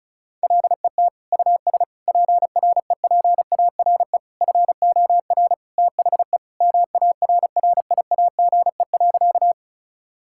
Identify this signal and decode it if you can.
Morse code, heard as a keyed tone, about 36 wpm, 700 Hz. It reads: LET US PREPARE FOR THE MARRIAGE.
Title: Sentences from Top 1000 Words - Review Part 6 - 35wpm